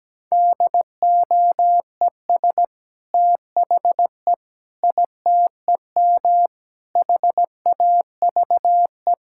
Morse code, heard as a keyed tone, about 17 wpm, 700 Hz. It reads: DOES THE ITEM HAVE
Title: Sentences from Top 1000 Words - Review Part 6 - 17wpm